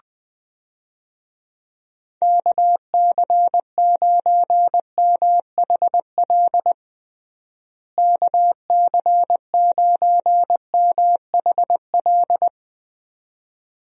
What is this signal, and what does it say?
Morse code, heard as a keyed tone, about 21 wpm, 700 Hz. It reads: KC9MHL KC9MHL
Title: USA Call Signs - 6 Characters - Repeated 2x - 20wpm